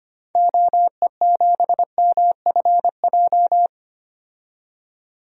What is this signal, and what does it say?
Morse code, heard as a keyed tone, about 25 wpm, 700 Hz. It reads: OE7MFJ
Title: DXCC-100 Call Signs - 6 Characters - 25wpm